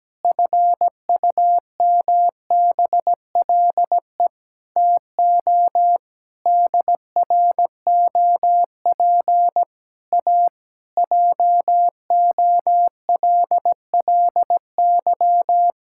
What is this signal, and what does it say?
Morse code, heard as a keyed tone, about 17 wpm, 700 Hz. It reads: FUMBLE TO DROP A JOLLY
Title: Pangram Challenge v2 - 17wpm